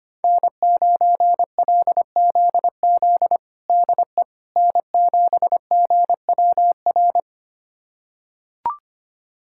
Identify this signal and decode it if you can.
Morse code, heard as a keyed tone, about 25 wpm, 700 Hz. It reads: N9LZZ DE N7GWR
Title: USA Callsign DE Callsign - 5 Characters - 25wpm